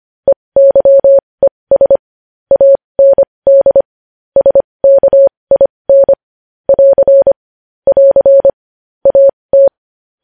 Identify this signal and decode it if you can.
Morse code, heard as a keyed tone, about 25 wpm, spaced as for 20 wpm, 550 Hz. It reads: EYES AND SKIN <AR> <AR> AT
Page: news-headlines-in-morse-code-at-20-wpm